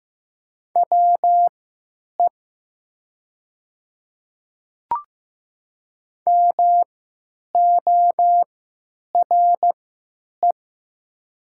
Morse code, heard as a keyed tone, about 15 wpm, spaced as for 8 wpm, 700 Hz. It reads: WE MORE
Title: 3x Character Spacing - Top 100 Words - 15wpm